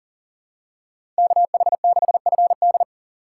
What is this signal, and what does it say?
Morse code, heard as a keyed tone, about 39 wpm, 700 Hz. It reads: KH6FD
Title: USA Call Signs - 5 Characters - 4x Character Spacing - 40wpm